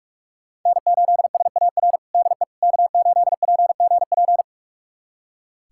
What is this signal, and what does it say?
Morse code, heard as a keyed tone, about 41 wpm, 700 Hz. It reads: N8SAR DE K8PGP
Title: USA Callsign DE Callsign - 5 Characters - 45wpm